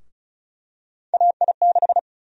Morse code, heard as a keyed tone, about 34 wpm, 700 Hz. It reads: AI6
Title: USA Call Sign Prefixes - 35wpm